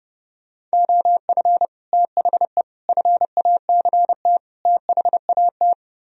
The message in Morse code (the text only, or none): OF THE FACT THAT